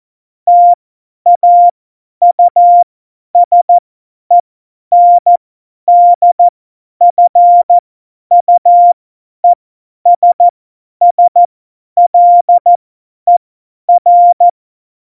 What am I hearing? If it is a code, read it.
Morse code, heard as a keyed tone, about 14 wpm, spaced as for 10 wpm, 700 Hz. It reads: TAUSENDFUESSLER
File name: Morsekurs / Witze / Witz 22. Mai 2018, 50 F70